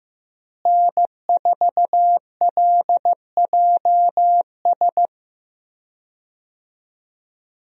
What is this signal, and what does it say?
Morse code, heard as a keyed tone, about 15 wpm, 700 Hz. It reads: N4LJS
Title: USA Call Signs - 5 Characters - Repeated 3x - 15wpm